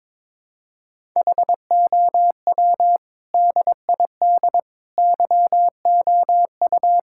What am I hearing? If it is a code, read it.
Morse code, heard as a keyed tone, about 22 wpm, 700 Hz. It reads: HOW DID YOU
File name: Sentences with 7-Word Phrases - 22wpm